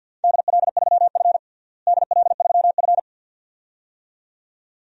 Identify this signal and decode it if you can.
Morse code, heard as a keyed tone, about 51 wpm, 700 Hz. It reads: DL3F DL3F
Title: DXCC-100 Call Signs - 4 Characters - Repeated 2x - 50wpm